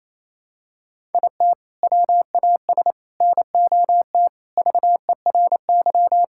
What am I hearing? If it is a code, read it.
Morse code, heard as a keyed tone, about 28 wpm, 700 Hz. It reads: IT WAS NOT VERY
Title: Sentences from Top 1000 Words - Review Part 5 - 28wpm